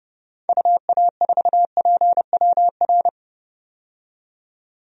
Morse code, heard as a keyed tone, about 31 wpm, 700 Hz. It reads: UA4PWR
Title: DXCC-100 Call Signs - 6 Characters - 30wpm